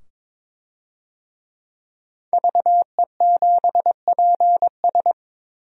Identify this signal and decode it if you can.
Morse code, heard as a keyed tone, about 22 wpm, 700 Hz. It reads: VE7PS
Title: DXCC-100 Call Signs - 5 Characters - 22wpm